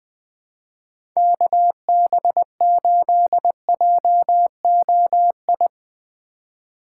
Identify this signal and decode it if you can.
Morse code, heard as a keyed tone, about 20 wpm, 700 Hz. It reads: KB8JOI
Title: USA Call Signs - 6 Characters - 2x Character Spacing - 20wpm